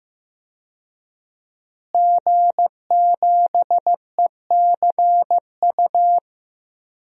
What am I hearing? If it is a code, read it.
Morse code, heard as a keyed tone, about 16 wpm, 700 Hz. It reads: G7ECU
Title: DXCC-100 Call Signs - 5 Characters - Repeated 3x - 15wpm